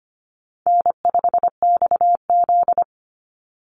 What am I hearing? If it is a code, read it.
Morse code, heard as a keyed tone, about 25 wpm, 700 Hz. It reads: N5XZ
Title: Easy Call Signs Repeated 2x - 25wpm